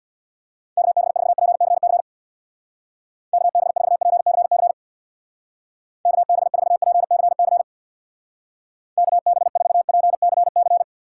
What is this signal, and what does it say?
Morse code, heard as a keyed tone, about 64 wpm, 700 Hz. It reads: KB4PCC KB4PCC KB4PCC KB4PCC